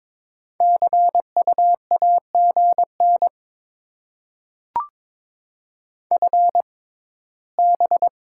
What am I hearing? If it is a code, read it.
Morse code, heard as a keyed tone, about 22 wpm, spaced as for 15 wpm, 700 Hz. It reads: CU AGN FB
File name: Top 20 QSO Elements - 6x Character Spacing - 22wpm